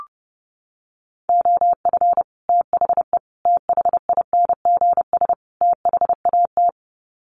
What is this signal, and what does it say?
Morse code, heard as a keyed tone, about 30 wpm, 700 Hz. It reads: OF THE THINGS THAT
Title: Sets of 4 Words 30wpm